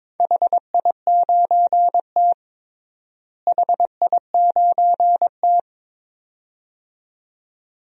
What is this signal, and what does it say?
Morse code, heard as a keyed tone, about 22 wpm, 700 Hz. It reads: HI9T HI9T
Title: DXCC-100 Call Signs - 4 Characters - Repeated 2x - 22wpm